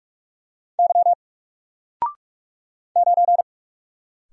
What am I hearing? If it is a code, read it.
Morse code, heard as a keyed tone, about 45 wpm, 700 Hz. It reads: Y 9
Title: Single Letter-Number Alt 45wpm